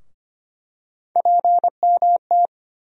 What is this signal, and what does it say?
Morse code, heard as a keyed tone, about 25 wpm, 700 Hz. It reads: PMT
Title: USA Call Sign Postfixes - 3 Letters 25wpm